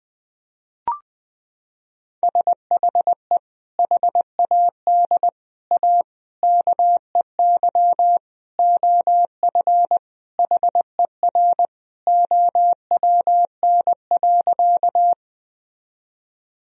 Morse code, heard as a keyed tone, about 20 wpm, 700 Hz. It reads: SHE HAD A KEY OF HER OWN.
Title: Sentences from Top 1000 Words - Review Part 4 - 20wpm